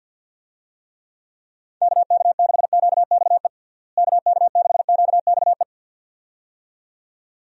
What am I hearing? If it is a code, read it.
Morse code, heard as a keyed tone, about 51 wpm, 700 Hz. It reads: KK6QXE KK6QXE
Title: USA Call Signs - 6 Characters - Repeated 2x - 50wpm